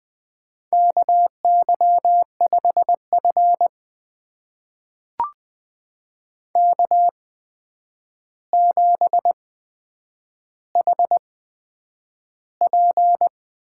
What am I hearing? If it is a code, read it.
Morse code, heard as a keyed tone, about 20 wpm, spaced as for 9 wpm, 700 Hz. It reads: KY5F K7HP
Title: USA Call Signs - 4 Characters - 8x Character Spacing - 20wpm